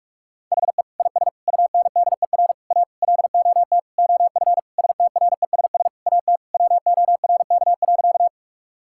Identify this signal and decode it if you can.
Morse code, heard as a keyed tone, about 42 wpm, 700 Hz. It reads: HE IS UNDER A LOT OF STRESS AT WORK.
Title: Sentences from Top 1000 Words - Review Part 6 - 45wpm